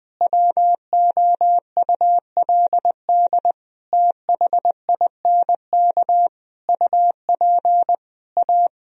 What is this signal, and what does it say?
Morse code, heard as a keyed tone, about 21 wpm, 700 Hz. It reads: WOULD THINK UP A
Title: Sentences from Top 1000 Words - Review Part 2 - 20wpm